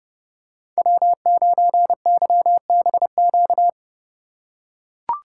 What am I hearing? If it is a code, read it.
Morse code, heard as a keyed tone, about 30 wpm, 700 Hz. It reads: W9YBQ
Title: USA Call Signs - 5 Characters - 3x Character Spacing - 30wpm